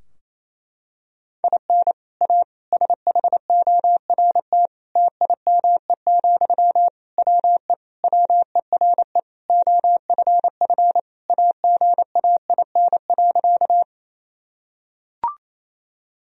Morse code, heard as a keyed tone, about 28 wpm, 700 Hz. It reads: IN A SHORT TIME, WE WERE OFF AGAIN.
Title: Sentences from Top 1000 Words - Review Part 2 - 28wpm